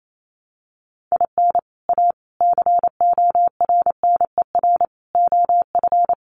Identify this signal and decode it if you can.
Morse code, heard as a keyed tone, about 28 wpm, 700 Hz. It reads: IN A CORNER OF
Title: Sets of 5 Words Encore2 28wpm